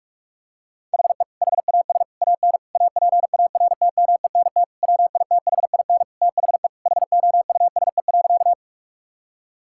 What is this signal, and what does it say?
Morse code, heard as a keyed tone, about 47 wpm, 700 Hz. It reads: HE HAS AN APARTMENT WITHIN THE HOUSE.
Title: Sentences from Top 1000 Words - Review Part 6 - 45wpm